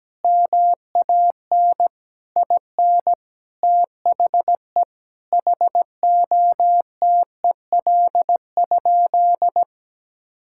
Morse code, heard as a keyed tone, about 17 wpm, 700 Hz. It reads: MAN IN THE HOTEL?
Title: Sentences from Top 1600 Words 17wpm